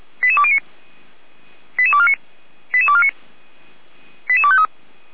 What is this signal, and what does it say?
Five-tone selective call (ZVEI), heard as quick runs of five tones, tones of about 70 ms. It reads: 88209 88249 88249 88253
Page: /455/SDR/ZVEI/88392/